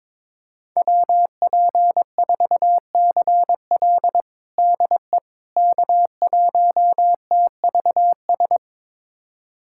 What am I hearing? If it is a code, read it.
Morse code, heard as a keyed tone, about 21 wpm, 700 Hz. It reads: WP4CL DE K1TVS
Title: USA Callsign DE Callsign - 5 Characters - 22wpm